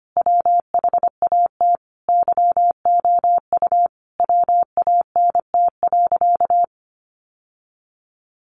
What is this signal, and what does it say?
Morse code, heard as a keyed tone, about 25 wpm, 700 Hz. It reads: WHAT YOU WANT.
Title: Sentences from Top 1100 Words 25wpm